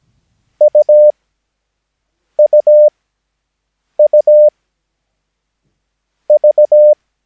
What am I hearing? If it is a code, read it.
Morse code, heard as a keyed tone, about 17 wpm, spaced as for 5 wpm, 600 Hz. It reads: UUUV
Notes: Övningsmaterial med alfabetet A till Ö samt siffror 0 till 9 enligt International/Continental Morse Code.